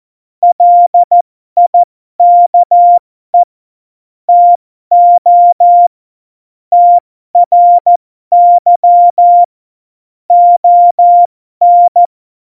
Morse code, heard as a keyed tone, about 14 wpm, spaced as for 12 wpm, 700 Hz. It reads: LIKE TO TRY ON
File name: Morsekurs / Witze